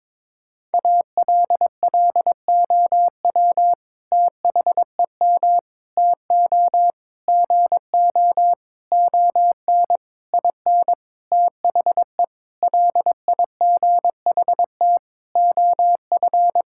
Morse code, heard as a keyed tone, about 22 wpm, 700 Hz. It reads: ALLOW THEM TO GO ON IN THE LIGHT OF
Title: Sentences from Top 1000 Words - Review Part 5 - 22wpm